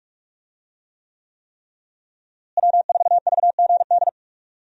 Morse code, heard as a keyed tone, about 46 wpm, 700 Hz. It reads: W4VGD
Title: USA Call Signs - 5 Characters - Repeated 3x - 45wpm